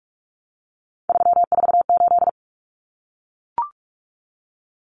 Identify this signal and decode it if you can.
Morse code, heard as a keyed tone, about 44 wpm, 700 Hz. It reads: 348
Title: Common Cut Numbers - 3-Digit 45wpm